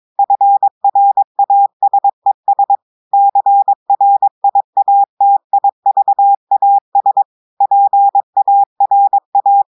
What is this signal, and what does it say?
Morse code, heard as a keyed tone, about 23 wpm, 800 Hz. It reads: FRASES CRIATIVAS PARA
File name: audios / portugais / portugais texte en clair 2 vitesse 22 mots minutes